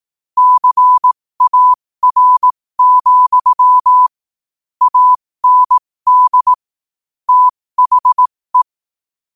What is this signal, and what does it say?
Morse code, heard as a keyed tone, about 18 wpm, spaced as for 15 wpm, 1000 Hz. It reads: CAR, AND THE
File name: qotdfiles / 15wpm